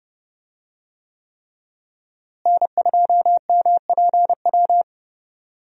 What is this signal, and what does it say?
Morse code, heard as a keyed tone, about 30 wpm, 700 Hz. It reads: N2MPW
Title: USA Call Signs - 5 Characters - Repeated 3x - 30wpm